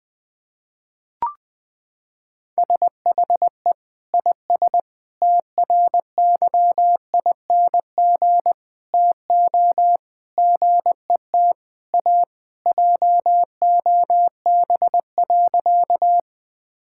Morse code, heard as a keyed tone, about 20 wpm, 700 Hz. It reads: SHE IS TRYING TO GET A JOB.
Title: Sentences with 3-Word Phrases - 20wpm